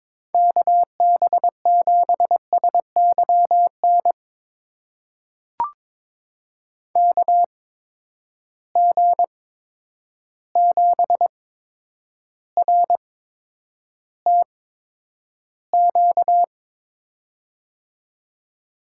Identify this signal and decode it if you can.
Morse code, heard as a keyed tone, about 22 wpm, spaced as for 9 wpm, 700 Hz. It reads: KB7SYN KG7RTQ
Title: USA Call Signs - 6 Characters - 8x Character Spacing - 22wpm